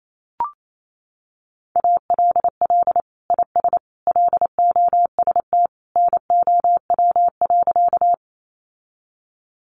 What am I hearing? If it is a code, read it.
Morse code, heard as a keyed tone, about 28 wpm, 700 Hz. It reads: ALL IS LOST NOW.
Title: Sentences from Top 1400 Words 28wpm